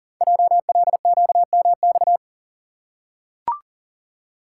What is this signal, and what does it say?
Morse code, heard as a keyed tone, about 38 wpm, 700 Hz. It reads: JLQMX
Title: Code Groups - 5 Letters - 40wpm